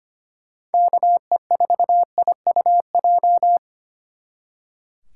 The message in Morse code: KE4IUJ

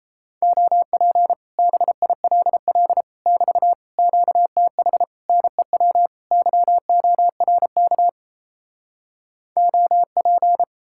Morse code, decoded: OP BILL <BT> QTH NEW YORK OP

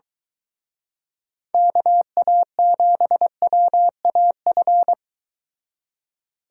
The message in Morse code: KA7WAF

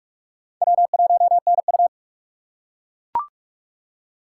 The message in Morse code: W1NU

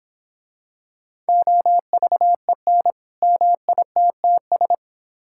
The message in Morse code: OVEN MITTS

